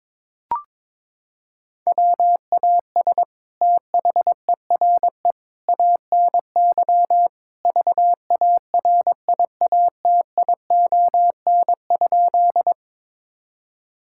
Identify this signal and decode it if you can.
Morse code, heard as a keyed tone, about 22 wpm, 700 Hz. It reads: WAS THERE ANY VARIATION?
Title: Sentences from Top 1600 Words 22wpm